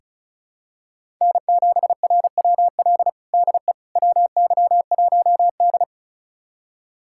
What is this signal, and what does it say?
Morse code, heard as a keyed tone, about 34 wpm, 700 Hz. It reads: N7RWL DE WY1D
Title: USA Callsign DE Callsign 35wpm